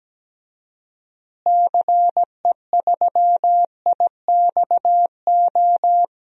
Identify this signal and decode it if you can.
Morse code, heard as a keyed tone, about 17 wpm, 700 Hz. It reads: CE3IXO
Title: DXCC-100 Call Signs - 6 Characters - Repeated 2x - 17wpm